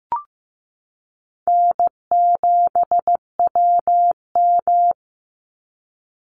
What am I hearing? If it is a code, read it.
Morse code, heard as a keyed tone, about 14 wpm, 700 Hz. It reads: N7WM
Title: NA SOTA Chaser Callsigns 15wpm